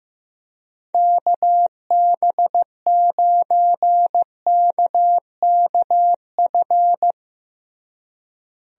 Morse code, heard as a keyed tone, about 15 wpm, 700 Hz. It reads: KB9KKF
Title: USA Call Signs - 6 Characters - Repeated 2x - 15wpm